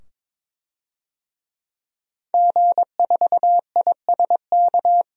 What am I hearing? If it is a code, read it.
Morse code, heard as a keyed tone, about 22 wpm, 700 Hz. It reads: G4ISK
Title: DXCC-100 Call Signs - 5 Characters - 22wpm